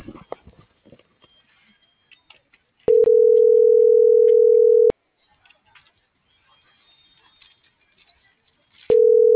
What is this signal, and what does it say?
Ringback tone (North America), 440 + 480 Hz.